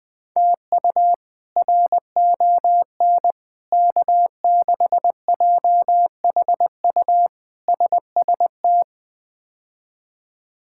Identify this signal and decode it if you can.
Morse code, heard as a keyed tone, about 20 wpm, 700 Hz. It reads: TU RON K6JHU SST
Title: K1USN SST QSO Practice - 20wpm